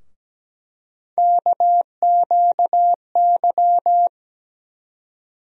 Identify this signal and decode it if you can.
Morse code, heard as a keyed tone, about 17 wpm, 700 Hz. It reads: KQY
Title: USA Call Sign Postfixes - 3 Letters 17wpm